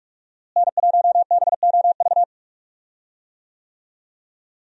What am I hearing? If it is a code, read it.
Morse code, heard as a keyed tone, about 42 wpm, 700 Hz. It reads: N1BOV
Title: USA Call Signs - 5 Characters - Repeated 2x - 45wpm